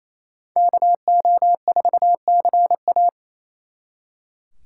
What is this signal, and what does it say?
Morse code, heard as a keyed tone, about 37 wpm, 700 Hz. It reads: KO4CA